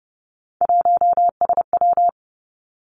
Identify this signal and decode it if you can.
Morse code, heard as a keyed tone, about 30 wpm, 700 Hz. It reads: 1SW
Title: Three Challenging Letter-Number Combinations 30wpm